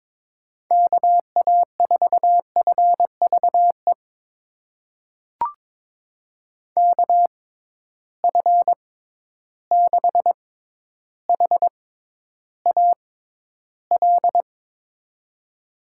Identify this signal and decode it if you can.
Morse code, heard as a keyed tone, about 21 wpm, spaced as for 11 wpm, 700 Hz. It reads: KA4FVE KF6HAL
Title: USA Call Signs - 6 Characters - 6x Character Spacing - 22wpm